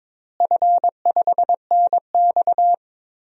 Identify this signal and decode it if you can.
Morse code, heard as a keyed tone, about 22 wpm, 700 Hz. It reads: F5NX
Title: DXCC-100 Call Signs - 4 Characters - Repeated 3x - 22wpm